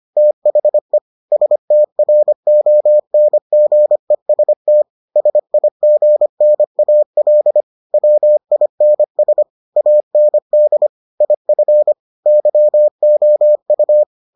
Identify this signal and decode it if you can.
Morse code, heard as a keyed tone, about 25 wpm, 600 Hz. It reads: THE STRONGEST SIGNAL WINS AND IF YOU